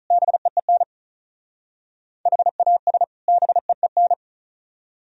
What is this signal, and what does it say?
Morse code, heard as a keyed tone, about 45 wpm, 700 Hz. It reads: BEEN HAS BEEN